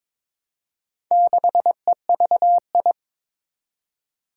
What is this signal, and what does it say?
Morse code, heard as a keyed tone, about 23 wpm, 700 Hz. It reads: 6EVI
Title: Four Challenging Letter-Number Combinations 22wpm